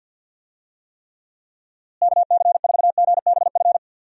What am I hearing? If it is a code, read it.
Morse code, heard as a keyed tone, about 50 wpm, 700 Hz. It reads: KC4GBF
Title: USA Call Signs - 6 Characters - Repeated 3x - 50wpm